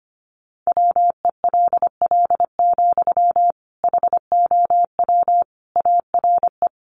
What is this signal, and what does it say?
Morse code, heard as a keyed tone, about 25 wpm, 700 Hz. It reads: WELL, HOW ARE
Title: Sentences from Top 200 Words 25wpm